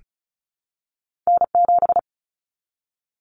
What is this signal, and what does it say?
Morse code, heard as a keyed tone, about 36 wpm, 700 Hz. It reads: N7
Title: Common Cut Numbers - 2-Digit 35wpm